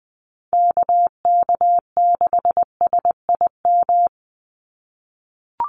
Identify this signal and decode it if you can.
Morse code, heard as a keyed tone, about 20 wpm, 700 Hz. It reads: KK6SIM
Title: USA Call Signs 20wpm